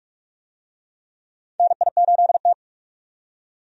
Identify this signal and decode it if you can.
Morse code, heard as a keyed tone, about 46 wpm, 700 Hz. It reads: NI8T
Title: USA Call Signs - 4 Characters - 45wpm